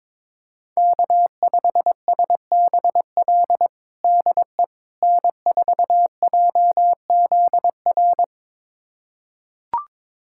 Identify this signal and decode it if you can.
Morse code, heard as a keyed tone, about 22 wpm, 700 Hz. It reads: K5SBL DE N4JZR
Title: USA Callsign DE Callsign - 5 Characters - 22wpm